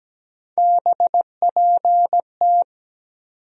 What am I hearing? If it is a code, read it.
Morse code, heard as a keyed tone, about 17 wpm, 700 Hz. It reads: BPT
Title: USA Call Sign Postfixes - 3 Letters 17wpm